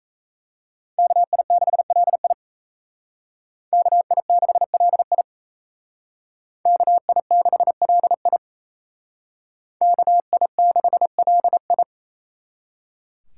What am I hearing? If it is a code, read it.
Morse code, heard as a keyed tone, about 35 wpm, 700 Hz. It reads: KI6LI KI6LI KI6LI KI6LI
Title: Speed Racing - USA Call Signs 28wpm